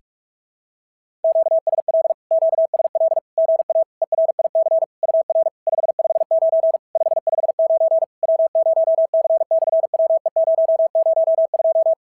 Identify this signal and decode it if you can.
Morse code, heard as a keyed tone, about 44 wpm, 650 Hz. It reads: QSL QSL GA ERIC UR 559 559 W0C/WE002
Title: SOTA QSO Practice 45wpm